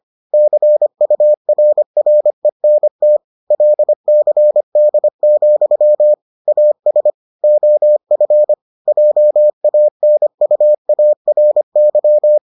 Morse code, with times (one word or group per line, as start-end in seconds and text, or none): 0.3-3.2 CURRENT
3.5-6.1 LCD,
6.5-7.1 AS
7.4-8.5 OF
8.9-12.4 JANUARY